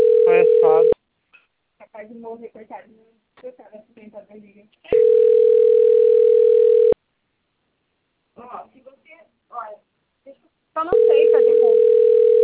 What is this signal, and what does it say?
Ringback tone (North America), 440 + 480 Hz, with speech.
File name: rg-1600-unknown- - - .26180